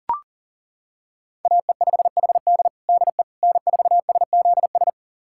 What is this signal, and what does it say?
Morse code, heard as a keyed tone, about 37 wpm, 700 Hz. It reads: AE5HD DE N4SZS
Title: USA Callsign DE Callsign - 5 Characters - 40wpm